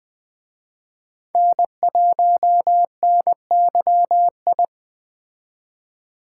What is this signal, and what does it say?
Morse code, heard as a keyed tone, about 20 wpm, 700 Hz. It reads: N1NYI
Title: Speed Racing - USA 1x3 Call Signs 17wpm